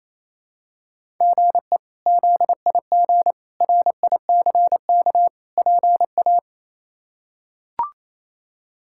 Morse code, heard as a keyed tone, about 28 wpm, 700 Hz. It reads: GE ZIG RICK PA
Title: K1USN SST QSO Practice - 28wpm